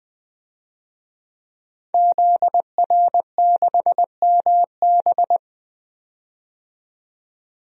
Morse code, heard as a keyed tone, about 20 wpm, 700 Hz. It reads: ZR6MB
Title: DXCC-100 Call Signs - 5 Characters - 20wpm